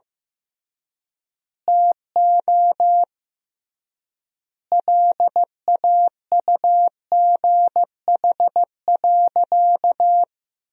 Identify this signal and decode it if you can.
Morse code, heard as a keyed tone, about 15 wpm, 700 Hz. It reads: TO LAUGH.
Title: 3x Word-Spacing - Sentences from Top 500 Words - 15wpm